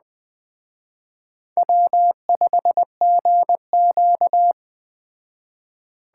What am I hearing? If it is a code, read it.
Morse code, heard as a keyed tone, about 20 wpm, 700 Hz. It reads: W5GQ